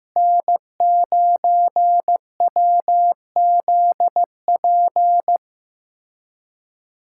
Morse code, heard as a keyed tone, about 15 wpm, 700 Hz. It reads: N9WZP
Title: USA Callsign DE Callsign - 5 Characters - 15wpm